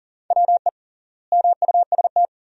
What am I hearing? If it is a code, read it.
Morse code, heard as a keyed tone, about 40 wpm, 700 Hz. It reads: WE MUST
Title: 3x Word-Spacing - Sentences from Top 300 Words - 40wpm